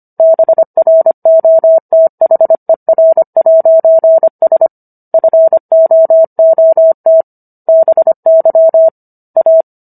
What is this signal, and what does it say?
Morse code, heard as a keyed tone, about 25 wpm, 650 Hz. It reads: BROTHER'S FOOT BY A